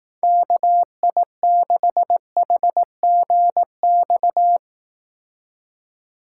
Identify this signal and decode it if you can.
Morse code, heard as a keyed tone, about 18 wpm, 700 Hz. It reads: KI6HGX
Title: Speed Racing - USA Call Signs 15wpm